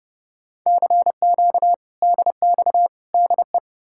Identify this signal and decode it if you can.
Morse code, heard as a keyed tone, about 40 wpm, spaced as for 35 wpm, 700 Hz. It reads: CQ DX DE
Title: Speed-Racing Common QSO Phrases Encore - 30wpm